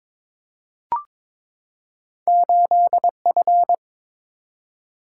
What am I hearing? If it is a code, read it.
Morse code, heard as a keyed tone, about 22 wpm, 700 Hz. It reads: 8F
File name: practice / 22wpm / Two Letter-Number Combinations - 22wpm